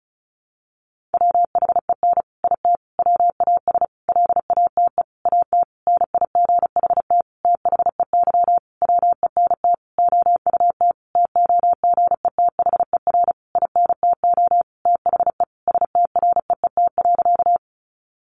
Morse code, heard as a keyed tone, about 34 wpm, 700 Hz. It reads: WHEN IT WAS LATE AT NIGHT THEY WENT OUT TOGETHER INTO THE STREET.
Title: Sentences from Top 600 Words 35wpm